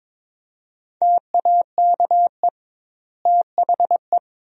Morse code, heard as a keyed tone, about 22 wpm, 700 Hz. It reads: TAKE THE
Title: 2x Word-Spacing - Sentences from Top 400 Words - 22wpm